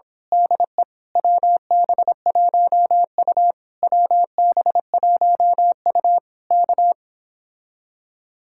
Morse code, heard as a keyed tone, about 26 wpm, 700 Hz. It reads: DE WB1U WB1U K
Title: Speed-Racing Common QSO Phrases - 22wpm